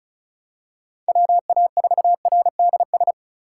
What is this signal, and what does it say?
Morse code, heard as a keyed tone, about 34 wpm, 690 Hz. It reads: WA4RDS
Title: USA Call Signs - 3x Character Spacing - 35wpm